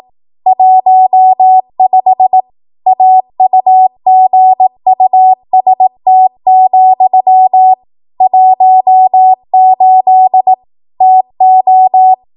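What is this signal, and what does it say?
Morse code, heard as a keyed tone, about 18 wpm, 750 Hz. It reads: 15 AUGUST, 18 TO
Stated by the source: ARRL 18 wpm Morse code practice archive transmission.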